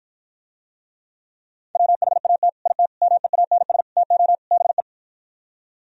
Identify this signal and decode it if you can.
Morse code, heard as a keyed tone, about 52 wpm, 700 Hz. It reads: WHAT IT MEANS TO BE